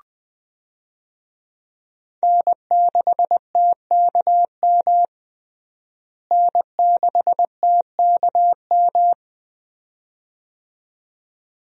Morse code, heard as a keyed tone, about 20 wpm, 700 Hz. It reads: N6TKM N6TKM
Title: USA Call Signs - 5 Characters - Repeated 2x - 20wpm